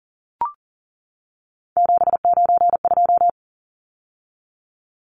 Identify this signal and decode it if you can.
Morse code, heard as a keyed tone, about 40 wpm, 700 Hz. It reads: 792
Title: Numbers - 3-Digit 40wpm